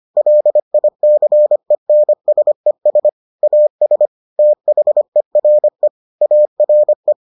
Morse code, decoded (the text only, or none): LICENSES AS THERE ARE